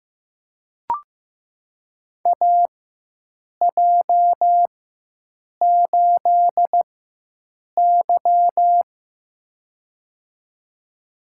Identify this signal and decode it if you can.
Morse code, heard as a keyed tone, about 15 wpm, spaced as for 7 wpm, 700 Hz. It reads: AJ8Y
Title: USA Call Signs - 4 Characters - 4x Character Spacing - 15wpm